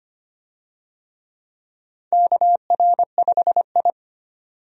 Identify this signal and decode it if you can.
Morse code, heard as a keyed tone, about 25 wpm, 700 Hz. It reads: KR5I
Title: USA Call Signs - 4 Characters - 25wpm